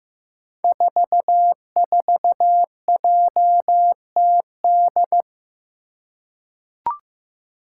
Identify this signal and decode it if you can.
Morse code, heard as a keyed tone, about 15 wpm, 700 Hz. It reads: 44JTD
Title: Five Challenging Letter-Number Combinations 15wpm